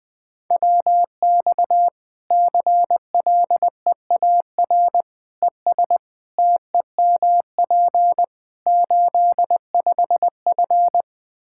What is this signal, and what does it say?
Morse code, heard as a keyed tone, about 20 wpm, 700 Hz. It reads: WX CLEAR ES TEMP 85F
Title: Practice QSO - Exchange 2 - Info - 20wpm